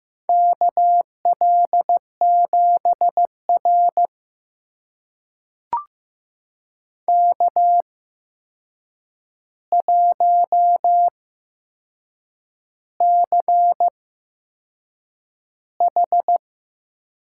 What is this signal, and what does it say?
Morse code, heard as a keyed tone, about 15 wpm, spaced as for 6 wpm, 700 Hz. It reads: KL7R K1CH
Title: USA Call Signs - 4 Characters - 8x Character Spacing - 15wpm